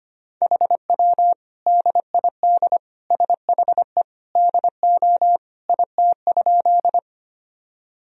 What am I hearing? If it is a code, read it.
Morse code, heard as a keyed tone, about 25 wpm, 700 Hz. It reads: HW DID SHE DO IT?
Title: Sentences from Top 100 Words and Abbreviations 25wpm